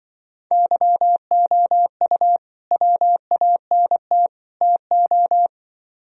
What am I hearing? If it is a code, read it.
Morse code, heard as a keyed tone, about 24 wpm, 700 Hz. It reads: YOU WANT TO
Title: Speed-Racing - Sets of 3 Words 20wpm